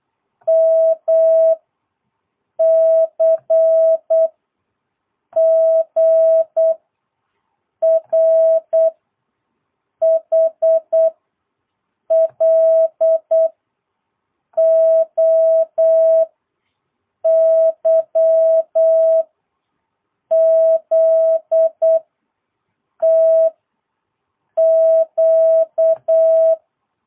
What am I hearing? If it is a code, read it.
Morse code, heard as a keyed tone, about 8 wpm, 650 Hz. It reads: M C G R H L O Y Z T Q